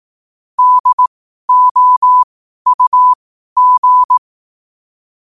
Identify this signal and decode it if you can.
Morse code, heard as a keyed tone, about 18 wpm, spaced as for 12 wpm, 1000 Hz. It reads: DOUG
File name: qotdfiles / 12wpm